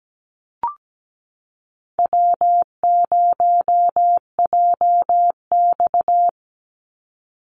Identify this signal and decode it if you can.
Morse code, heard as a keyed tone, about 17 wpm, 700 Hz. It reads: W0JX
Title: Easy Call Signs Repeated 3x - 17wpm